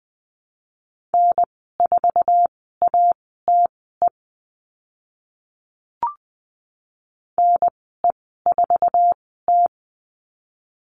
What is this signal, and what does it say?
Morse code, heard as a keyed tone, about 20 wpm, spaced as for 14 wpm, 700 Hz. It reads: N4ATE NE4T
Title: 062 Lesson 11g - Letter h - Call Signs 20wpm